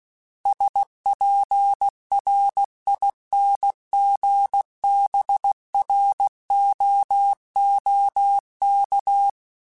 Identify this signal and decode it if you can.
Morse code, heard as a keyed tone, about 16 wpm, 800 Hz. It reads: SPRINGBROOK
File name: fromSMs / FromSMs mp3s / Cities@15